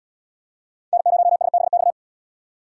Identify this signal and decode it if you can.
Morse code, heard as a keyed tone, about 68 wpm, 700 Hz. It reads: N0SBX